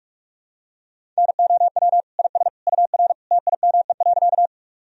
Morse code, heard as a keyed tone, about 41 wpm, 700 Hz. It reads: NOW IS UR TIME.
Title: Sentences from Top 100 Words and Abbreviations 45wpm